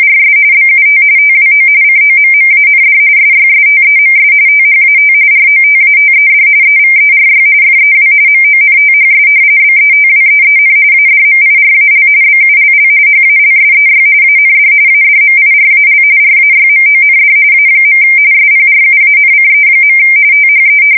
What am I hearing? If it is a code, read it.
RTTY, about 45.45 baud. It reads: RS BUT TWO COMMAND SYSTEMS THAT ROTATED EVERY FEW MONTHS BETWEEN ADMIRAL BULL HALSEY AND ADMIRAL RAYMOND A. SPRUANCE. THE
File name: fsk170 / usnavy4